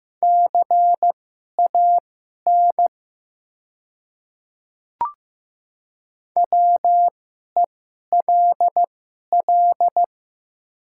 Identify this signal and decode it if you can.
Morse code, heard as a keyed tone, about 15 wpm, spaced as for 11 wpm, 700 Hz. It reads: CAN WELL
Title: 2x Character Spacing - Top 200 Words - 15wpm